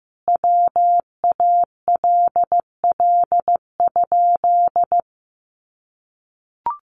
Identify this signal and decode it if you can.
Morse code, heard as a keyed tone, about 15 wpm, 700 Hz. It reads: WALL?
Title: Sentences from Top 1100 Words 15wpm